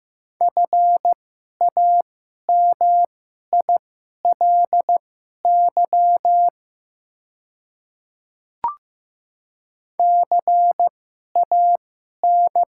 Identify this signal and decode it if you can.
Morse code, heard as a keyed tone, about 15 wpm, spaced as for 11 wpm, 700 Hz. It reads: FAMILY CAN
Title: 2x Character Spacing - Top 500 Words - 15wpm